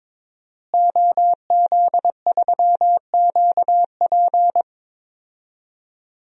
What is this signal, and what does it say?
Morse code, heard as a keyed tone, about 22 wpm, 700 Hz. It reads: OZ3QP
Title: DXCC-100 Call Signs - 5 Characters - Repeated 3x - 22wpm